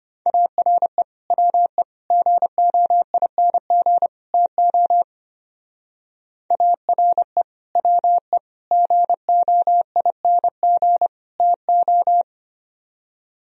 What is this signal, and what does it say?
Morse code, heard as a keyed tone, about 29 wpm, 700 Hz. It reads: ARE WE GOING TO ARE WE GOING TO
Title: Speed-Racing - Sets of 4 Words 25wpm